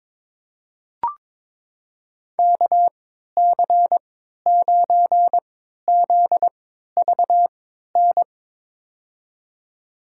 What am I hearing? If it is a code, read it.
Morse code, heard as a keyed tone, about 22 wpm, spaced as for 13 wpm, 700 Hz. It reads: KC9ZVN
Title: USA Call Signs - 6 Characters - 3x Character Spacing - 22wpm